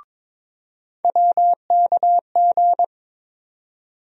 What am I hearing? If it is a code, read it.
Morse code, heard as a keyed tone, about 22 wpm, 700 Hz. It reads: WKG
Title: USA Call Sign Postfixes - 3 Letters 22wpm